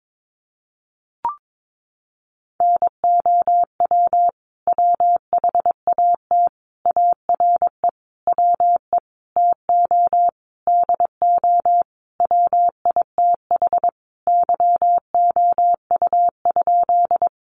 Morse code, heard as a keyed tone, about 22 wpm, 700 Hz. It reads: NOW WHAT ARE WE TO DO WITH YOU?
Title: Sentences from Top 100 Words 22wpm